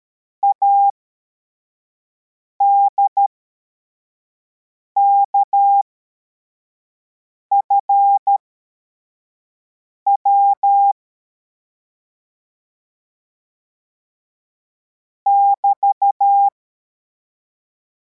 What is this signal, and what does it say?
Morse code, heard as a keyed tone, about 13 wpm, spaced as for 4 wpm, 800 Hz. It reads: ADKFW <BT>
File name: formation / cw / Text85